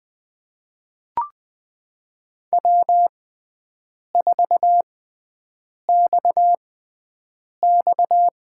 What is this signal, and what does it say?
Morse code, heard as a keyed tone, about 20 wpm, spaced as for 7 wpm, 700 Hz. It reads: W4XX
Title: USA Call Signs - 4 Characters - 6x Character Spacing - 20wpm